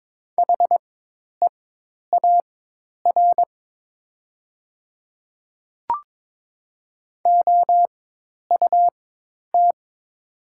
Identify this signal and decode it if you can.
Morse code, heard as a keyed tone, about 22 wpm, spaced as for 10 wpm, 700 Hz. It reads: HEAR OUT
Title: 4x Character Spacing - Top 300 Words - 22wpm